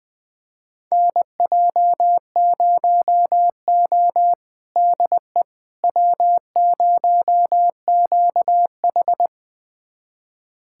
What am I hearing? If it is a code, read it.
Morse code, heard as a keyed tone, about 20 wpm, 700 Hz. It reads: NJ0O DE W0QH
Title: USA Callsign DE Callsign - 4 Characters - 20wpm